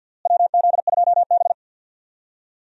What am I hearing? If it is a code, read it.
Morse code, heard as a keyed tone, about 46 wpm, 700 Hz. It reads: WZ2B
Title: USA Call Signs - 4 Characters - Repeated 2x - 50wpm